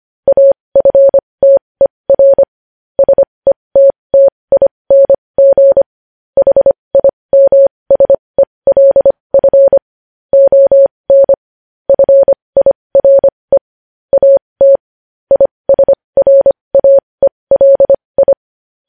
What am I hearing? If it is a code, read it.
Morse code, heard as a keyed tone, about 25 wpm, spaced as for 20 wpm, 550 Hz. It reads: AFTER SETTING HIMSELF ON FIRE AT ISRAELI